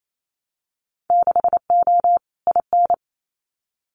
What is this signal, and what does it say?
Morse code, heard as a keyed tone, about 28 wpm, 700 Hz. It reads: 6O IN
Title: Winter Field Day Exchanges Encore 28wpm